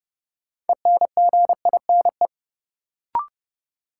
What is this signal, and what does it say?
Morse code, heard as a keyed tone, about 30 wpm, 700 Hz. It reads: ENGINE
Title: Sentences from Top 1500 Words - New Word Drill Alt - 30wpm